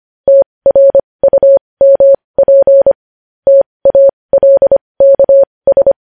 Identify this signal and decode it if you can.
Morse code, heard as a keyed tone, about 25 wpm, spaced as for 20 wpm, 550 Hz. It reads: TRUMP TALKS
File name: podcast / News headlines-1x-25x20